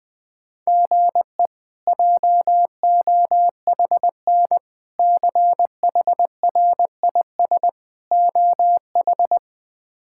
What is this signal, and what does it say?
Morse code, heard as a keyed tone, about 21 wpm, 700 Hz. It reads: GE JOHN CHRIS OH
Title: K1USN SST QSO Practice - 20wpm